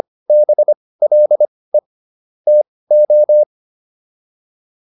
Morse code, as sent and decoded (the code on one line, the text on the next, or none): -... .-.. / - ---
BLE TO